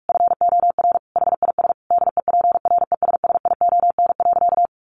Morse code, heard as a keyed tone, about 44 wpm, 700 Hz. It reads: FOR HIS DEPRESSION.
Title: Sentences from Top 1300 Words 45wpm